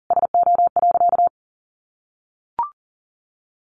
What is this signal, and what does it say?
Morse code, heard as a keyed tone, about 40 wpm, 700 Hz. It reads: SO.
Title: Sentences from Top 1500 Words 40wpm